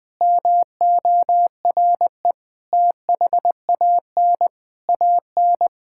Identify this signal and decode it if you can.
Morse code, heard as a keyed tone, about 20 wpm, 700 Hz. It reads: MORE THAN AN
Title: Sentences with 3-Word Phrases - Spoken Beginning - 20wpm